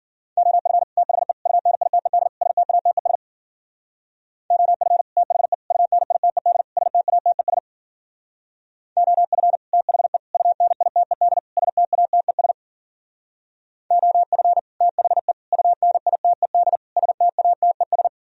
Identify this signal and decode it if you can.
Morse code, heard as a keyed tone, about 49 wpm, 700 Hz. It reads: OF THE UNITED STATES OF THE UNITED STATES OF THE UNITED STATES OF THE UNITED STATES
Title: Speed-Racing - Sets of 4 Words 40wpm